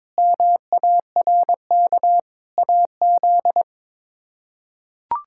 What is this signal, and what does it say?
Morse code, heard as a keyed tone, about 22 wpm, 700 Hz. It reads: MARK AZ
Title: K1USN SST QSO Practice - 22wpm